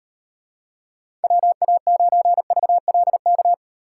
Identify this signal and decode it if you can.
Morse code, heard as a keyed tone, about 38 wpm, 700 Hz. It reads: WA9VLK